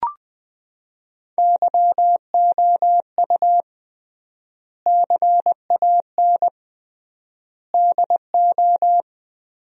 Morse code, heard as a keyed tone, about 20 wpm, 700 Hz. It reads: YOU CAN DO
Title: 3x Word-Spacing - Sentences from Top 100 Words - 20wpm